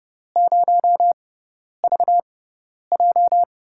Morse code, heard as a keyed tone, about 31 wpm, spaced as for 10 wpm, 700 Hz. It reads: 0VJ
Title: USA Call Signs - 6x Character Spacing - 30wpm